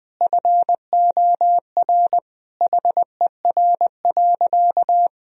Morse code, heard as a keyed tone, about 20 wpm, 700 Hz. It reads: FOR HER.
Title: Sentences from Top 1000 Words - Review Part 3 - 20wpm